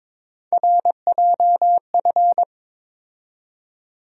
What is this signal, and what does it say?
Morse code, heard as a keyed tone, about 22 wpm, 700 Hz. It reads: RJF
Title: USA Call Sign Postfixes - 3 Letters 22wpm